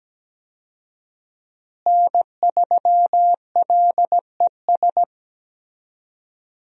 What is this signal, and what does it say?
Morse code, heard as a keyed tone, about 17 wpm, 700 Hz. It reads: N3LES
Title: USA Call Signs - 5 Characters - Repeated 2x - 17wpm